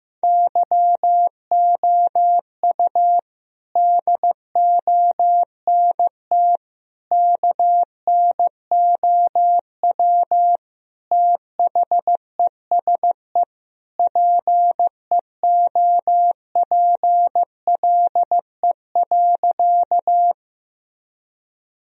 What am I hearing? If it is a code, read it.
Morse code, heard as a keyed tone, about 15 wpm, 700 Hz. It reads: YOU DONT KNOW THESE PEOPLE.
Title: Sentences from Top 1000 Words - Review Part 1 - 15wpm